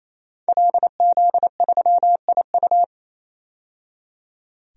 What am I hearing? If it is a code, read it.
Morse code, heard as a keyed tone, about 28 wpm, 700 Hz. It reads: LZ3IU
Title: DXCC-100 Call Signs - 5 Characters - 28wpm